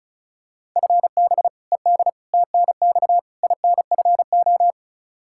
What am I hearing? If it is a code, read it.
Morse code, heard as a keyed tone, about 36 wpm, 700 Hz. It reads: FB ED TNX INFO
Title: K1USN SST QSO Practice 35wpm